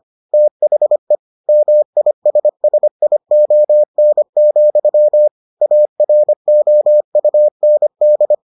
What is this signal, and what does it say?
Morse code, heard as a keyed tone, about 25 wpm, 600 Hz. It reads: THE MISSION, AROUND